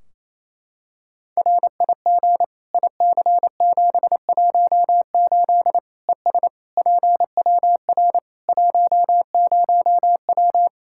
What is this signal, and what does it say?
Morse code, heard as a keyed tone, about 28 wpm, 700 Hz. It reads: RIG IC718 ES PWR 10W
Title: Practice QSO - Exchange 2 - Info - 28wpm